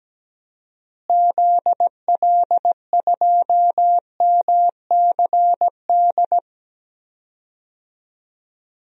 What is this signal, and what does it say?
Morse code, heard as a keyed tone, about 17 wpm, 700 Hz. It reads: ZL2MCD
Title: DXCC-100 Call Signs - 6 Characters - Repeated 2x - 17wpm